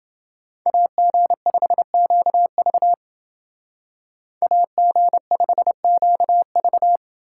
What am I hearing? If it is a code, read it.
Morse code, heard as a keyed tone, about 29 wpm, 700 Hz. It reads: AG5QV AG5QV